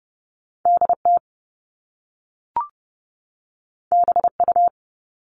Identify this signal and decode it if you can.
Morse code, heard as a keyed tone, about 30 wpm, 700 Hz. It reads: D T B U
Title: All Cut Numbers - 2-Digit 30wpm